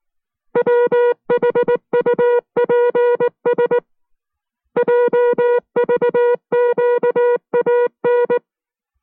Morse code, heard as a keyed tone, about 19 wpm, 480 Hz. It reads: WHUPS JVQAN